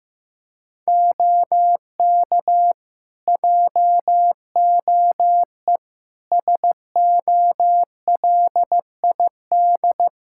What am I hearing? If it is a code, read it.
Morse code, heard as a keyed tone, about 15 wpm, 700 Hz. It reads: OK JOE SOLID
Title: Practice QSO - Exchange 3 - More Info - 15wpm